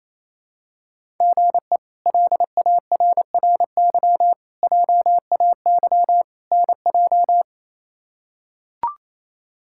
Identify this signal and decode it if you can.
Morse code, heard as a keyed tone, about 28 wpm, 700 Hz. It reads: GE LARRY JAY NJ
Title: K1USN SST QSO Practice - 28wpm